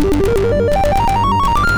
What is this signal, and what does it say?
There is no code.